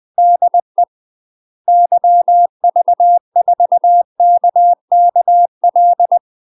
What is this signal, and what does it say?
Morse code, heard as a keyed tone, about 20 wpm, 700 Hz. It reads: DE YV4KKL
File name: UK MASTER QSOs TRAINING / 20WPM / UK MASTER QSOs 0049